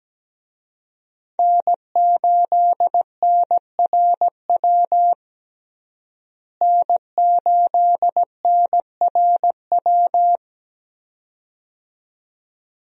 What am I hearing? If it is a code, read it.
Morse code, heard as a keyed tone, about 17 wpm, 700 Hz. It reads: N8NRW N8NRW
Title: USA Call Signs - 5 Characters - Repeated 2x - 17wpm